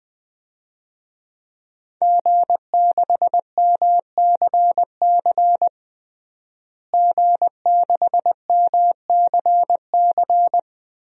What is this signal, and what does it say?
Morse code, heard as a keyed tone, about 21 wpm, 700 Hz. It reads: G6MCC G6MCC
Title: DXCC-100 Call Signs - 5 Characters - Repeated 2x - 20wpm